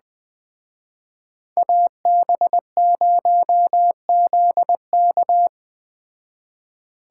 Morse code, heard as a keyed tone, about 19 wpm, 700 Hz. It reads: AB0ZK